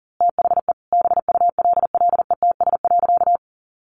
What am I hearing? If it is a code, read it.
Morse code, heard as a keyed tone, about 40 wpm, 700 Hz. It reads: THE BULLETS.
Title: A Sound Of Thunder 40wpm